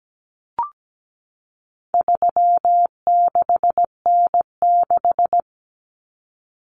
Morse code, heard as a keyed tone, about 17 wpm, 700 Hz. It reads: 36N6
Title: Common Cut Numbers - 4-Digit 17wpm